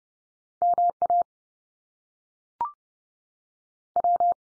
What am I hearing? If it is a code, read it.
Morse code, heard as a keyed tone, about 30 wpm, 700 Hz. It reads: MA W
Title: US States 30WPM